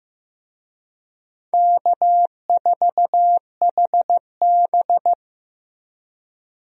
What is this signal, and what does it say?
Morse code, heard as a keyed tone, about 15 wpm, 700 Hz. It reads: K4HB
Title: USA Call Signs - 4 Characters - Repeated 3x - 15wpm